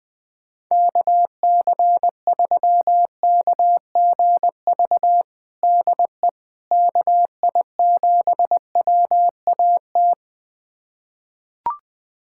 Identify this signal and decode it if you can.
Morse code, heard as a keyed tone, about 21 wpm, 700 Hz. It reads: KC3KGV DE KI7WAT
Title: USA Callsign DE Callsign 20wpm